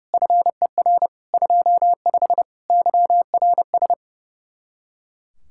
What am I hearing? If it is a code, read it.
Morse code, heard as a keyed tone, about 36 wpm, 700 Hz. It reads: FER 25 YRS
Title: Speed-Racing Common QSO Phrases Encore - 30wpm